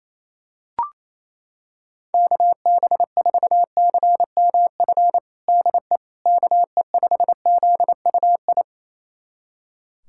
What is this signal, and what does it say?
Morse code, heard as a keyed tone, about 27 wpm, 700 Hz. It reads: KB4CMF DE KE5ZUI
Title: USA Callsign DE Callsign 28wpm